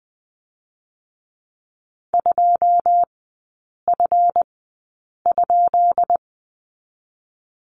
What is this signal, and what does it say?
Morse code, heard as a keyed tone, about 20 wpm, 700 Hz. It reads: 2 F ?
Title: 184 Lesson 30e - Letter b - Three Characters 20wpm